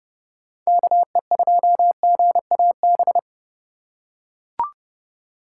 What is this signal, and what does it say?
Morse code, heard as a keyed tone, about 30 wpm, 700 Hz. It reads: KE2GAB
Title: USA Call Signs - 6 Characters - 8x Character Spacing - 30wpm